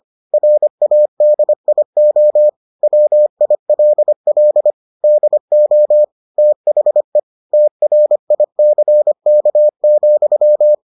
Morse code, heard as a keyed tone, about 25 wpm, 600 Hz. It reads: RADIO WILL DO THE TRICK,